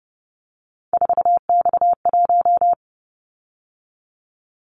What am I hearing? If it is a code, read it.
Morse code, heard as a keyed tone, about 30 wpm, 700 Hz. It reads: 4X1
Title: Callsign Prefixes for Top 50 DXCC - 30wpm